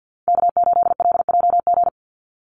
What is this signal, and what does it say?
Morse code, heard as a keyed tone, about 50 wpm, 700 Hz. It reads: K8LJG
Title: NA SOTA Chaser Callsigns 50wpm